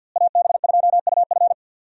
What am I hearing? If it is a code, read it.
Morse code, heard as a keyed tone, about 47 wpm, 700 Hz. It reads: AB2UF
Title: USA Call Signs - 5 Characters - Repeated 2x - 50wpm